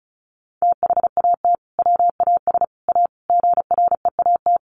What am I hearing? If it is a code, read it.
Morse code, heard as a keyed tone, about 34 wpm, 700 Hz. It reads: THAT WAS A GREAT
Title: Sentences from Top 1000 Words 35wpm